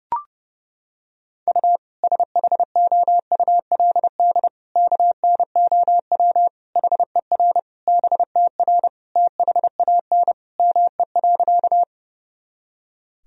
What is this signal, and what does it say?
Morse code, heard as a keyed tone, about 32 wpm, 700 Hz. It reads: U SHOULD KNOW HER BTR THAN ME.
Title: Sentences from Top 400 Words and Abbreviations 30wpm